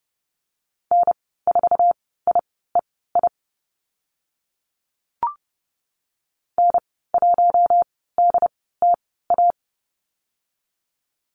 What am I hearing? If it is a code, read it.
Morse code, heard as a keyed tone, about 30 wpm, spaced as for 17 wpm, 700 Hz. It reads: N4IEI N1DTA
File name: course / 30:10wpm / 088 Lesson 15g - Number 5 - Call Signs 30wpm